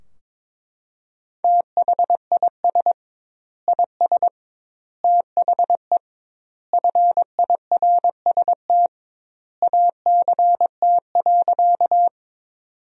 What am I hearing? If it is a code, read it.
Morse code, heard as a keyed tone, about 22 wpm, 700 Hz. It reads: THIS IS THE FIRST ACT.
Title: 2x Word-Spacing - Sentences from Top 200 Words - 22wpm